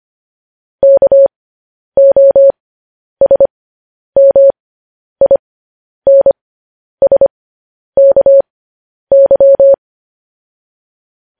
Morse code, heard as a keyed tone, about 25 wpm, spaced as for 10 wpm, 550 Hz. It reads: KOSMINSKY
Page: news-headlines-in-morse-code-at-10-wpm